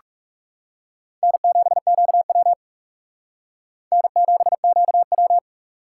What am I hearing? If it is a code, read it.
Morse code, heard as a keyed tone, about 37 wpm, 700 Hz. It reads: N7QW N7QW
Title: Speed Racing - USA Call Signs 30wpm